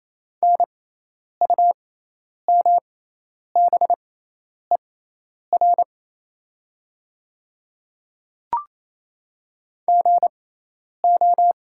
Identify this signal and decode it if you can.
Morse code, heard as a keyed tone, about 28 wpm, spaced as for 10 wpm, 700 Hz. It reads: NUMBER GO
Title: 6x Character Spacing - Top 100 Words - 28wpm